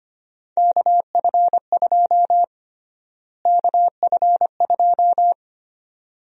Morse code, heard as a keyed tone, about 25 wpm, 700 Hz. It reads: KF2 KF2
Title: USA Call Sign Prefixes - Repeated 2x - 25wpm